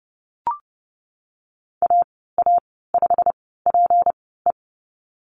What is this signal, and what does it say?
Morse code, heard as a keyed tone, about 30 wpm, spaced as for 17 wpm, 700 Hz. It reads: AA5PE
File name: Lesson 37g - Letter x - Call Signs 30wpm